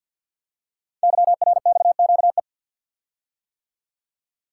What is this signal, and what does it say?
Morse code, heard as a keyed tone, about 48 wpm, 700 Hz. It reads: YRXQE
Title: Code Groups - 5 Letters - 50wpm